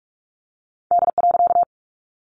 Morse code, heard as a keyed tone, about 45 wpm, 700 Hz. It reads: D.
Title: A Sound Of Thunder 45wpm